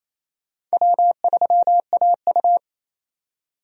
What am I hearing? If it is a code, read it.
Morse code, heard as a keyed tone, about 28 wpm, 700 Hz. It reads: W3AU
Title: USA Call Signs - 4 Characters - 28wpm